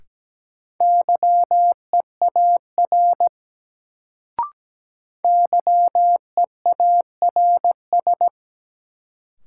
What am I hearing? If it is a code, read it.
Morse code, heard as a keyed tone, about 17 wpm, 700 Hz. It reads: YEAR YEARS
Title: Sentences from Top 200 Words - New Word Drill - 17wpm